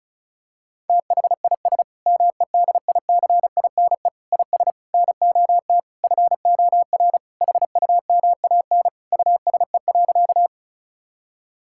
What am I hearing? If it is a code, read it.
Morse code, heard as a keyed tone, about 36 wpm, 700 Hz. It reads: THIS MEDICINE IS NOT FOR HUMAN USE.
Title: Sentences from Top 1000 Words - Review Part 6 - 35wpm